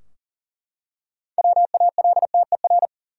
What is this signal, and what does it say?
Morse code, heard as a keyed tone, about 40 wpm, 700 Hz. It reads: WALTER
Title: USA Common Names - 40wpm